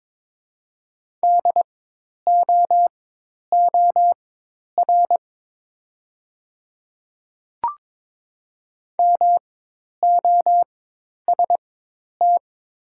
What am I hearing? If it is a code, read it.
Morse code, heard as a keyed tone, about 23 wpm, spaced as for 10 wpm, 700 Hz. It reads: DOOR MOST
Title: 4x Character Spacing - Top 300 Words - 22wpm